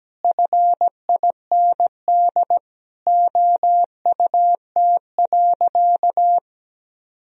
Morse code, heard as a keyed tone, about 17 wpm, 700 Hz. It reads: FIND OUT.
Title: Sentences from Top 200 Words and Abbreviations 17wpm